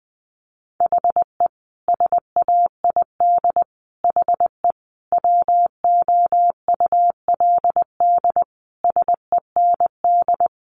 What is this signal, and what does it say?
Morse code, heard as a keyed tone, about 20 wpm, 700 Hz. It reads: HE SAID HE WOULD SEND
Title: Sentences from Top 1500 Words 20wpm